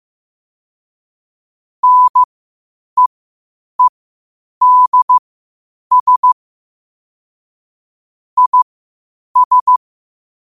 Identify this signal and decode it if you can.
Morse code, heard as a keyed tone, about 15 wpm, spaced as for 8 wpm, 1000 Hz. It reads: NEEDS IS
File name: qotdfiles / 8wpm